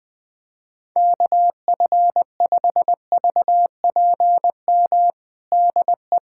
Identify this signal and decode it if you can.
Morse code, heard as a keyed tone, about 20 wpm, 700 Hz. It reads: KF5VPM DE
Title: USA Callsign DE Callsign - 6 Characters - 20wpm